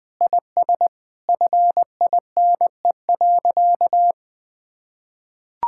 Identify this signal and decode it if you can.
Morse code, heard as a keyed tone, about 20 wpm, 700 Hz. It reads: IS FINE.
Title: Sentences with 6-Word Phrases - 20wpm